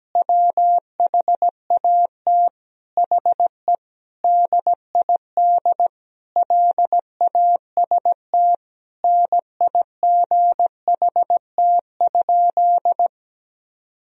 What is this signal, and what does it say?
Morse code, heard as a keyed tone, about 17 wpm, 700 Hz. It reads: WHAT HE DID LAST NIGHT?
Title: Sentences from Top 300 Words and Abbreviations 17wpm